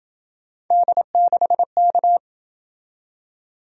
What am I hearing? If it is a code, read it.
Morse code, heard as a keyed tone, about 27 wpm, 700 Hz. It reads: D6K